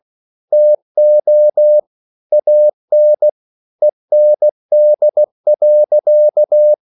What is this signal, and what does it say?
Morse code, heard as a keyed tone, about 16 wpm, 600 Hz. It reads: TO AN END.
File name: CW/Practice